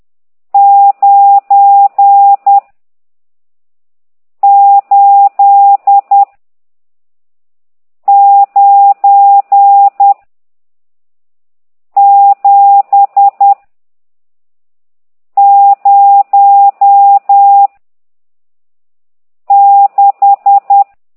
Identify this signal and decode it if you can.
Morse code, heard as a keyed tone, about 10 wpm, 800 Hz. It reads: 9 8 9 7 0 6
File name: ct2idw / exerc14